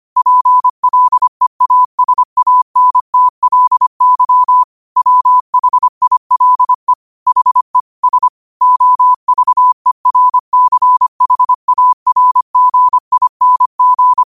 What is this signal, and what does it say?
Morse code, heard as a keyed tone, about 25 wpm, 1000 Hz. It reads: PLEASANTLY WHILE HE S OVERCHARGING